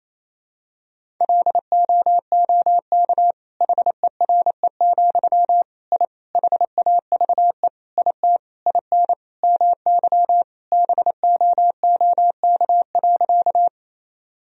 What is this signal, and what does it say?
Morse code, heard as a keyed tone, about 28 wpm, 700 Hz. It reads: LOOK HERE, I HAVE IT IN MY BOOK.
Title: Sentences from Top 1000 Words - Review Part 1 - 28wpm